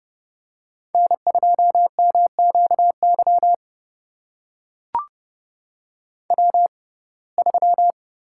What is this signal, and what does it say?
Morse code, heard as a keyed tone, about 30 wpm, spaced as for 22 wpm, 700 Hz. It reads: N2MQY W3
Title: USA Call Signs - 5 Characters - 6x Character Spacing - 30wpm